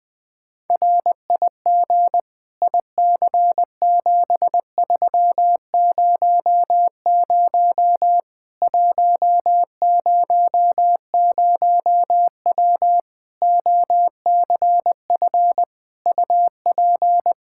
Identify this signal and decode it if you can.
Morse code, heard as a keyed tone, about 20 wpm, 700 Hz. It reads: RIG IC7300 100W OCF UP
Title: Practice QSO - Exchange 2 - Info - 20wpm